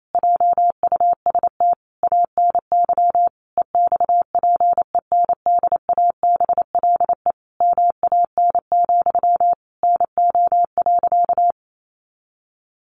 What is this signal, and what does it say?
Morse code, heard as a keyed tone, about 28 wpm, 700 Hz. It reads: JUST ANY EXPENDABLE MAN, NO.
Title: A Sound Of Thunder 28wpm